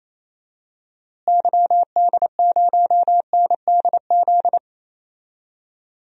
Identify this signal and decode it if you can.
Morse code, heard as a keyed tone, about 28 wpm, 700 Hz. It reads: YD0NDZ
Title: DXCC-100 Call Signs - 6 Characters - Repeated 3x - 28wpm